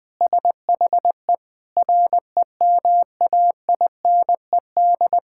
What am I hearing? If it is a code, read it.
Morse code, heard as a keyed tone, about 21 wpm, 700 Hz. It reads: SHE REMAINED
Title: Sentences with 5-Word Phrases - Spoken Beginning - 20wpm